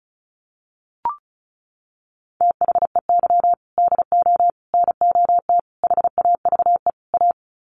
Morse code, heard as a keyed tone, about 34 wpm, 700 Hz. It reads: THEY DO NOT HAVE A
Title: Sets of 5 Words 35wpm